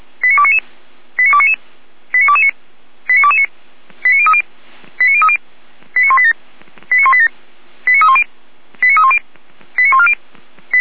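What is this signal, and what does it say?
Five-tone selective call (ZVEI), heard as quick runs of five tones, tones of about 70 ms. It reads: 89299 89299 89229 89229 89030 89030 89187 89187 89310 89310 89140
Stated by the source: ZVEI five-tone selective call received on an SDR.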